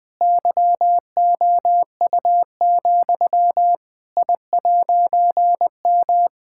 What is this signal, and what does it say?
Morse code, heard as a keyed tone, about 20 wpm, 700 Hz. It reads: YOU, I'M
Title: All-Age Appropriate Jokes - 20wpm